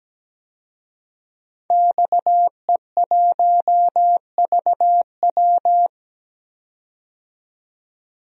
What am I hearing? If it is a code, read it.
Morse code, heard as a keyed tone, about 17 wpm, 700 Hz. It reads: XE1VW
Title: DXCC-100 Call Signs - 5 Characters - 17wpm